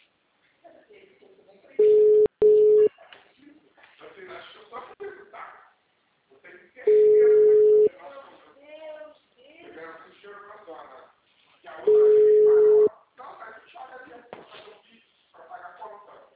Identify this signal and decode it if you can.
Ringback tone (Europe, 425 Hz), with speech.